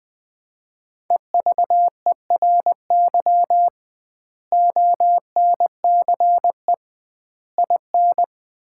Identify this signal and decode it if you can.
Morse code, heard as a keyed tone, about 21 wpm, 700 Hz. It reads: EVERY ONCE IN
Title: 2x Word-Spacing - Sets of 5 Words - 20wpm